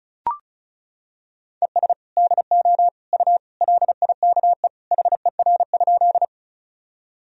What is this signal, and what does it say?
Morse code, heard as a keyed tone, about 34 wpm, 700 Hz. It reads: ES DO U LIKE HER?
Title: Sentences from Top 100 Words and Abbreviations 35wpm